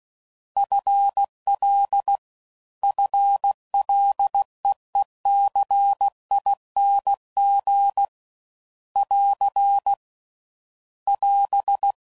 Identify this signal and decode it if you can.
Morse code, heard as a keyed tone, about 16 wpm, 800 Hz. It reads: FL FLEECING <AR> <AS>
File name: Prefixes / FLs & THs / FL Prefixes 1